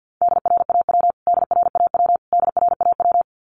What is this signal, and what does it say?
Morse code, heard as a keyed tone, about 49 wpm, 700 Hz. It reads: DRAW DRAW DRAW
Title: Top 500 Words Repeated 50wpm